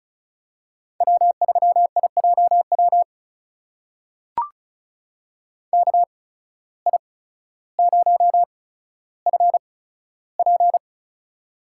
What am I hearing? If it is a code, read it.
Morse code, heard as a keyed tone, about 36 wpm, spaced as for 15 wpm, 700 Hz. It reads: W3IJW KI0FP
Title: USA Call Signs - 5 Characters - 8x Character Spacing - 35wpm